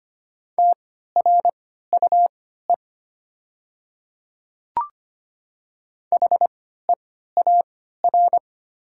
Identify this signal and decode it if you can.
Morse code, heard as a keyed tone, about 25 wpm, spaced as for 14 wpm, 690 Hz. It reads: TRUE HEAR
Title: 3x Character Spacing - Top 500 Words - 25wpm